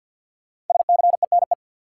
Morse code, heard as a keyed tone, about 52 wpm, 700 Hz. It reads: SCENE
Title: Sentences from Top 1100 Words - New Word Drill Alt - 50wpm